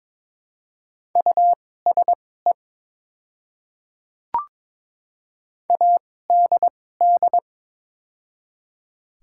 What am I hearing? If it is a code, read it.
Morse code, heard as a keyed tone, about 22 wpm, spaced as for 16 wpm, 700 Hz. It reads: USE ADD
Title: 2x Character Spacing - Top 200 Words - 22wpm